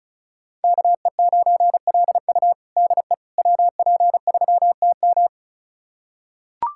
Morse code, heard as a keyed tone, about 36 wpm, 700 Hz. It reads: KE9LU DE WP3TM
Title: USA Callsign DE Callsign - 5 Characters - 35wpm